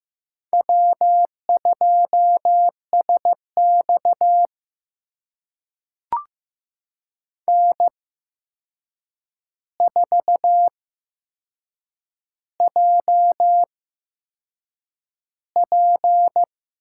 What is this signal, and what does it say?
Morse code, heard as a keyed tone, about 15 wpm, spaced as for 6 wpm, 700 Hz. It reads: W2SX N4JP
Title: USA Call Signs - 4 Characters - 8x Character Spacing - 15wpm